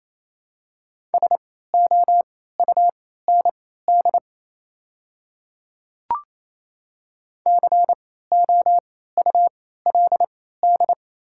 Morse code, heard as a keyed tone, about 28 wpm, spaced as for 16 wpm, 700 Hz. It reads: SOUND COULD
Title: 3x Character Spacing - Top 100 Words - 28wpm